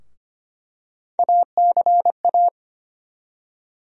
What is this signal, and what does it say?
Morse code, heard as a keyed tone, about 25 wpm, 700 Hz. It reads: ACA
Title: USA Call Sign Postfixes - 3 Letters 25wpm